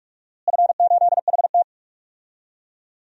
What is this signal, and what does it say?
Morse code, heard as a keyed tone, about 42 wpm, 700 Hz. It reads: F8HT
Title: DXCC-100 Call Signs - 4 Characters - Repeated 2x - 45wpm